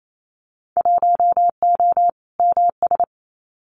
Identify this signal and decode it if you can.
Morse code, heard as a keyed tone, about 28 wpm, 700 Hz. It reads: 1O MS
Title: Winter Field Day Exchanges 28wpm